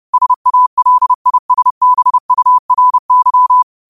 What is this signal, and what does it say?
Morse code, heard as a keyed tone, about 30 wpm, 1000 Hz. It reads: SALISBURY
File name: qotdfiles / 30wpm